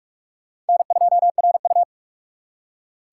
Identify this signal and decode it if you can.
Morse code, heard as a keyed tone, about 55 wpm, 700 Hz. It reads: N2RU
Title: Speed Racing - USA 1x2 Call Signs 45wpm